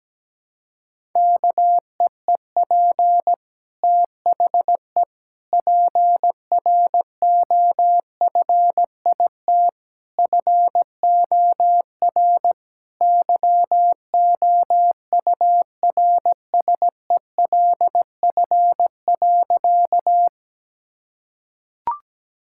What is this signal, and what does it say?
Morse code, heard as a keyed tone, about 17 wpm, 700 Hz. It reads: KEEP THE PROFIT FOR YOURSELF.
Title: Sentences from Top 1000 Words - Review Part 5 - 17wpm